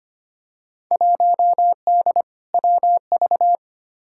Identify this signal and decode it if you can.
Morse code, heard as a keyed tone, about 25 wpm, 700 Hz. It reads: 1D WV
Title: Field Day Exchanges - 25wpm